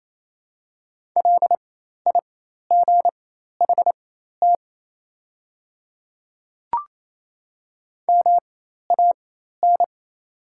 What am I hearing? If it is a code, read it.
Morse code, heard as a keyed tone, about 28 wpm, spaced as for 13 wpm, 700 Hz. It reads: LIGHT MAN
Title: 4x Character Spacing - Top 200 Words - 28wpm